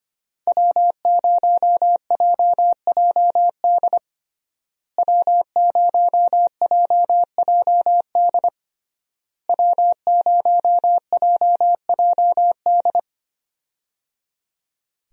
Morse code, heard as a keyed tone, about 25 wpm, 700 Hz. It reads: W0JJD W0JJD W0JJD
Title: USA Call Signs - 5 Characters - Repeated 3x - 25wpm